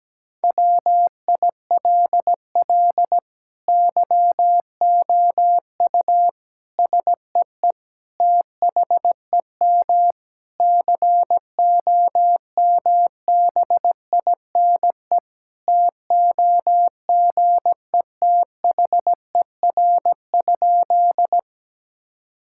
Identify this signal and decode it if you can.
Morse code, heard as a keyed tone, about 17 wpm, 700 Hz. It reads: WILL YOU SEE THEM COMBINE TOGETHER?
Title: Sentences from Top 1600 Words 17wpm